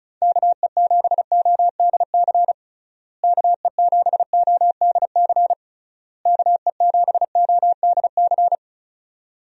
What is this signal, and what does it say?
Morse code, heard as a keyed tone, about 34 wpm, 700 Hz. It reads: KE7ODC KE7ODC KE7ODC
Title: USA Call Signs - 6 Characters - Repeated 3x - 35wpm